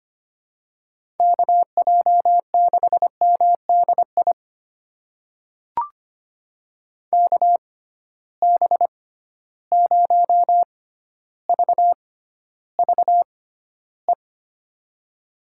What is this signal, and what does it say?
Morse code, heard as a keyed tone, about 25 wpm, spaced as for 13 wpm, 700 Hz. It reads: KJ6MDI KB0VVE
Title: USA Call Signs - 6 Characters - 6x Character Spacing - 25wpm